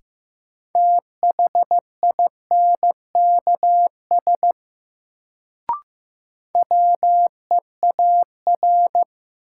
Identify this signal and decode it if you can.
Morse code, heard as a keyed tone, about 15 wpm, 700 Hz. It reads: THINKS WEAR
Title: Sentences from Top 1000 Words - Review Part 6 - New Word Drill - 15wpm